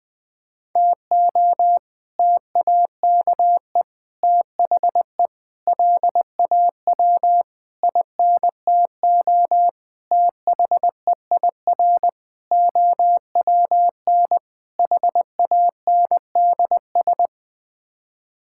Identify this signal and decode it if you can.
Morse code, heard as a keyed tone, about 20 wpm, 700 Hz. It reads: TO TAKE THE LAW INTO THEIR OWN HANDS
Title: Sets of 8 Words - 20wpm